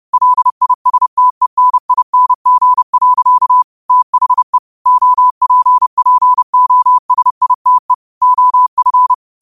Morse code, heard as a keyed tone, about 30 wpm, 1000 Hz. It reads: LISTENING. THE OPPOSITE OF
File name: 30wpm